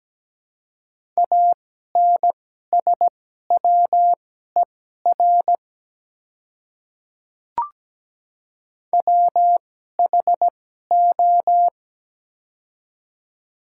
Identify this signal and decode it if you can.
Morse code, heard as a keyed tone, about 17 wpm, spaced as for 12 wpm, 700 Hz. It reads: ANSWER WHO
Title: 2x Character Spacing - Top 300 Words - 17wpm